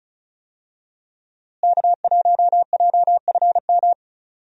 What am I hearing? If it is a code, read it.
Morse code, heard as a keyed tone, about 34 wpm, 700 Hz. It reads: K1JFM
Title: USA Call Signs - 5 Characters - Repeated 2x - 35wpm